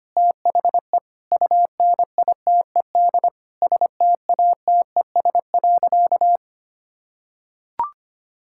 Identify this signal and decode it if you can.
Morse code, heard as a keyed tone, about 25 wpm, 700 Hz. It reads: THE UNITED STATES.
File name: practice / 25wpm / Sentences with 4-Word Phrases - 25wpm